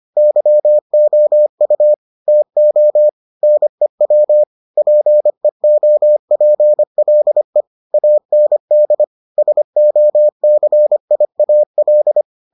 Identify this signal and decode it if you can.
Morse code, heard as a keyed tone, about 25 wpm, 600 Hz. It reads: YOU TO NEW PEOPLE AND SOCIAL